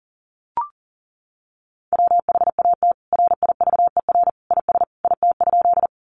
Morse code, heard as a keyed tone, about 40 wpm, 700 Hz. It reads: WHAT RIVER IS IT?
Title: Sentences from Top 300 Words 40wpm